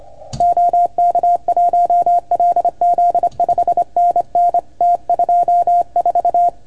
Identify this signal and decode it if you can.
Morse code, heard as a keyed tone, about 26 wpm, 690 Hz. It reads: OK1LZ 5NN T24
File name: Archiv / CUC310 / OK1FGD CUC 310 pileup